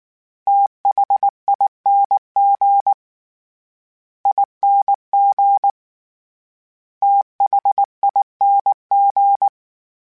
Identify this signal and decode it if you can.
Morse code, heard as a keyed tone, about 19 wpm, 800 Hz. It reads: THING ING THING
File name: Special INGs / Special INGs-mp3s / SlowToFaster ing @18